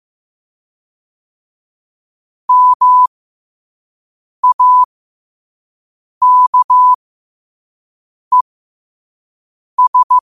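Morse code, heard as a keyed tone, about 15 wpm, spaced as for 5 wpm, 1000 Hz. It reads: MAKES
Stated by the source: Morse code at 5 wpm.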